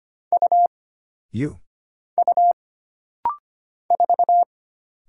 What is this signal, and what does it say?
Morse code, heard as a keyed tone, about 25 wpm, 700 Hz. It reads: U U 4
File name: Single Character (Letter:Number:Punctuation:Prosign) - Rapid-Fire - 25wpm